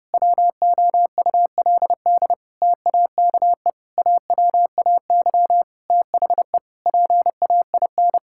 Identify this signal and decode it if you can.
Morse code, heard as a keyed tone, about 30 wpm, 700 Hz. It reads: WOULD TAKE AWAY THE PAIN
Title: Sentences from Top 1600 Words 30wpm